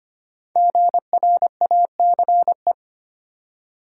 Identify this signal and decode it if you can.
Morse code, heard as a keyed tone, about 25 wpm, 700 Hz. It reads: GRACE
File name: 5 Letter Words - First 3 Letters Given - 25wpm